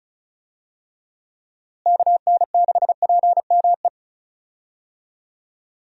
Morse code, heard as a keyed tone, about 36 wpm, 700 Hz. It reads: KN6PME
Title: USA Call Signs - 6 Characters - Repeated 2x - 35wpm